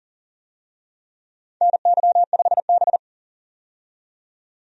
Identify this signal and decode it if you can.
Morse code, heard as a keyed tone, about 40 wpm, 700 Hz. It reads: NY5B
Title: USA Call Signs - 4 Characters - 40wpm